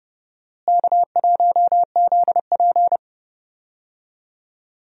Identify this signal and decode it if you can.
Morse code, heard as a keyed tone, about 30 wpm, 700 Hz. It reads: K1ZP
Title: USA Call Signs - 4 Characters - Repeated 3x - 30wpm